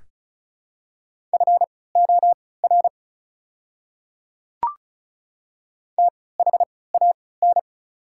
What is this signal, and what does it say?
Morse code, heard as a keyed tone, about 36 wpm, spaced as for 20 wpm, 700 Hz. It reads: FOR THAN
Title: 3x Character Spacing - Top 200 Words - 35wpm